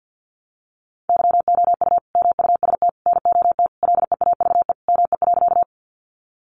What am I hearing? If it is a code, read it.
Morse code, heard as a keyed tone, about 50 wpm, 700 Hz. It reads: YOU MUST NOT LEAVE ME.
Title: Sentences from Top 600 Words 50wpm